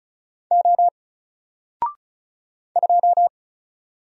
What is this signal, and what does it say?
Morse code, heard as a keyed tone, about 35 wpm, 700 Hz. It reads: O 2
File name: Single-Letter-Number-35wpm